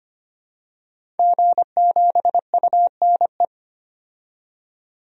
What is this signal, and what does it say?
Morse code, heard as a keyed tone, about 25 wpm, 700 Hz. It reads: G7UNE
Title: DXCC-100 Call Signs - 5 Characters - 25wpm